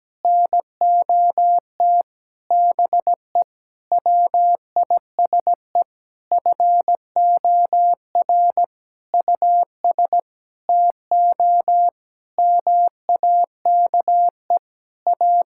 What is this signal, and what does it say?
Morse code, heard as a keyed tone, about 17 wpm, 700 Hz. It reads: NOT BE WISE FOR US TO MAKE A
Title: Sentences from Top 1600 Words 17wpm